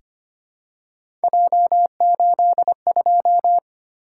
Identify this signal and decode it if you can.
Morse code, heard as a keyed tone, about 25 wpm, 700 Hz. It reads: J82
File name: Three Letter-Number Combinations - 25wpm